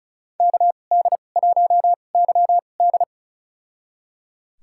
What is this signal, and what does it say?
Morse code, heard as a keyed tone, about 36 wpm, spaced as for 25 wpm, 700 Hz. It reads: KD1YD
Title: USA Call Signs - 5 Characters - 2x Character Spacing - 35wpm